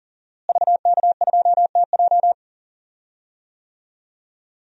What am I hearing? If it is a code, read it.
Morse code, heard as a keyed tone, about 40 wpm, 700 Hz. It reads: VK2TJ
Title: DXCC-100 Call Signs - 5 Characters - Repeated 2x - 40wpm